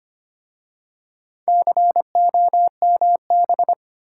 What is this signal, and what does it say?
Morse code, heard as a keyed tone, about 25 wpm, 690 Hz. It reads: COMB
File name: practice / 25wpm / Easy Words with Hints - 25wpm